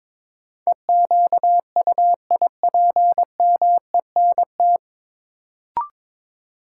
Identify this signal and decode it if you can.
Morse code, heard as a keyed tone, about 21 wpm, 700 Hz. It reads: EQUIPMENT
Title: Sentences from Top 700 Words - New Word Drill - 22wpm